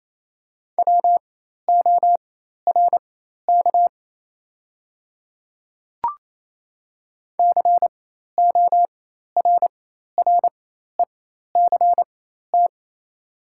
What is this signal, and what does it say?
Morse code, heard as a keyed tone, about 28 wpm, spaced as for 13 wpm, 690 Hz. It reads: WORK CORRECT
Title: 4x Character Spacing - Top 500 Words - 28wpm